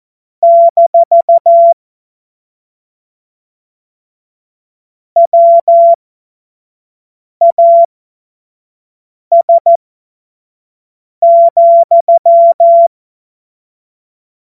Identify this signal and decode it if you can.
Morse code, heard as a keyed tone, about 14 wpm, spaced as for 5 wpm, 680 Hz. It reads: - WAS,
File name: Morsekurs / Witze / Witz 19 Dezember 2023 25 F70